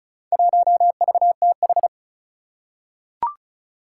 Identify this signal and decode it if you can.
Morse code, heard as a keyed tone, about 36 wpm, 700 Hz. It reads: 1VTH
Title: Four Challenging Letter-Number Combinations 35wpm